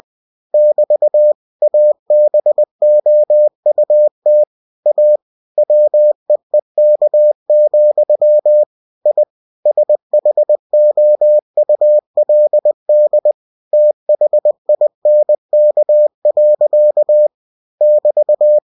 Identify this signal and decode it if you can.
Morse code, heard as a keyed tone, about 20 wpm, 600 Hz. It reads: <BT> ABOUT A WEEK, I SHOULD THINK. <BT>